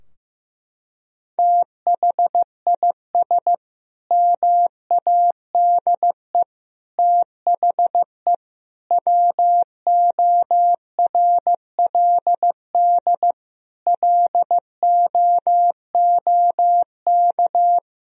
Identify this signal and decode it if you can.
Morse code, heard as a keyed tone, about 15 wpm, 700 Hz. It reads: THIS MADE THE WORLD LOOK
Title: Sentences from Top 1000 Words - Review Part 2 - 15wpm